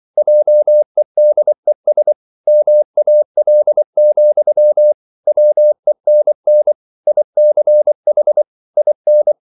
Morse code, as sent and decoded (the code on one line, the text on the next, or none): .--- . -.. ... / -- .- .-.. --..-- / .-- . -. -. / .. -.-. .... / .. -.
JEDES MAL, WENN ICH IN